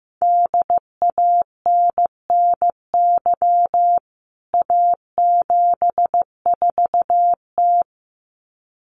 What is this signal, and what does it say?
Morse code, heard as a keyed tone, about 15 wpm, 700 Hz. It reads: DANNY A74T
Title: CWT Exchange 15wpm